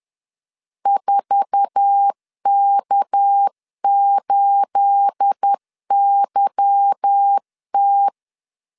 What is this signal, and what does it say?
Morse code, heard as a keyed tone, about 10 wpm, 800 Hz. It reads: 4K8YT
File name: LEZIONI / 9 28 10wpm 2842